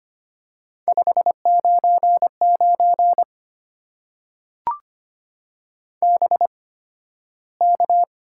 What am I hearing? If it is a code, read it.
Morse code, heard as a keyed tone, about 25 wpm, spaced as for 13 wpm, 700 Hz. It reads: 599 BK
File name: Top 90 QSO Elements - 8x Character Spacing - 25wpm